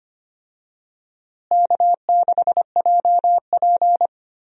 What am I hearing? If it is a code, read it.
Morse code, heard as a keyed tone, about 25 wpm, 700 Hz. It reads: K6JP
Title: USA Call Signs - 4 Characters - Repeated 3x - 25wpm